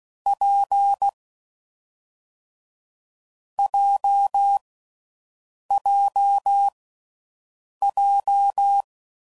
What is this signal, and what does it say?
Morse code, heard as a keyed tone, about 16 wpm, 800 Hz. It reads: P J J J
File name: Combinations / Difficult Combos / WPJ WPJ1